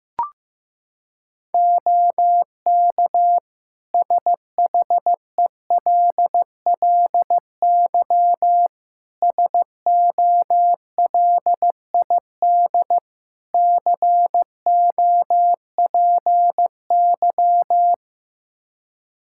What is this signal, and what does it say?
Morse code, heard as a keyed tone, about 15 wpm, 700 Hz. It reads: OK SHELLY SOLID COPY
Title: Practice QSO - Exchange 3 - More Info - 15wpm